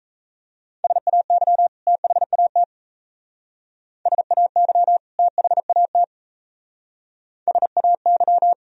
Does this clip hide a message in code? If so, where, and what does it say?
Yes: Morse code from 0.8 to 8.5 s, heard as a keyed tone, about 35 wpm, 700 Hz. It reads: SAY THAT SAY THAT SAY